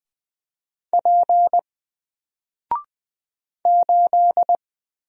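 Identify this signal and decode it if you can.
Morse code, heard as a keyed tone, about 20 wpm, 700 Hz. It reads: P 8
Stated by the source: Single letter and number Morse drill at 20 wpm, from N4LCD.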